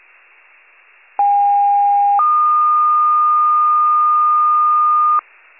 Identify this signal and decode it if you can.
Two-tone page, heard as a one-second tone, then a three-second tone, about 810 then 1220 Hz. It is heard from 1.2 to 5.2 s.
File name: pager / 73 RESCUE / wcalert